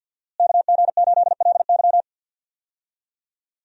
Morse code, heard as a keyed tone, about 47 wpm, 700 Hz. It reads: KG8LY
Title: USA Call Signs - 5 Characters - Repeated 3x - 50wpm